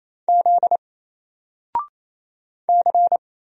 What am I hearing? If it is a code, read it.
Morse code, heard as a keyed tone, about 27 wpm, 700 Hz. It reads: Z C
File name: Single-Letter-Number-28wpm